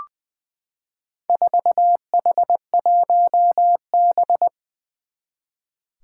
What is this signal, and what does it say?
Morse code, heard as a keyed tone, about 21 wpm, 700 Hz. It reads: 4H1B
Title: Four Challenging Letter-Number Combinations 20wpm